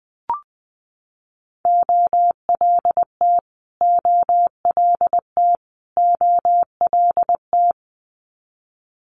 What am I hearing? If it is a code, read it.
Morse code, heard as a keyed tone, about 20 wpm, 700 Hz. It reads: OLT OLT OLT
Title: Multi-letter Phonemes Repeated 3x 20wpm